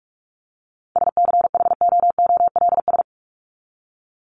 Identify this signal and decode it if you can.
Morse code, heard as a keyed tone, about 44 wpm, 700 Hz. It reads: SCHOOLS
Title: ES:S Suffix - Top 100 Words with Base Words 45wpm